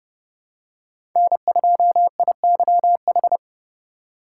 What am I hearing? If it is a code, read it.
Morse code, heard as a keyed tone, about 32 wpm, 700 Hz. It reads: N2IYH
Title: USA Call Signs - 5 Characters - 2x Character Spacing - 30wpm